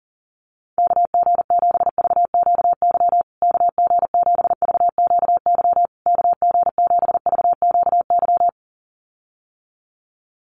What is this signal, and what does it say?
Morse code, heard as a keyed tone, about 40 wpm, 700 Hz. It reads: KG7VQY KG7VQY KG7VQY
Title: USA Call Signs Repeated 3x 40wpm